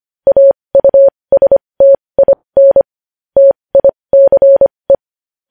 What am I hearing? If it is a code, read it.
Morse code, heard as a keyed tone, about 25 wpm, spaced as for 20 wpm, 550 Hz. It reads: AUSTIN TICE
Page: news-headlines-in-morse-code-at-20-wpm